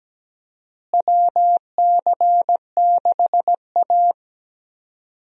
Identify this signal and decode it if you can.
Morse code, heard as a keyed tone, about 17 wpm, 700 Hz. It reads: WC6A
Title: USA Call Signs - 4 Characters - Repeated 3x - 17wpm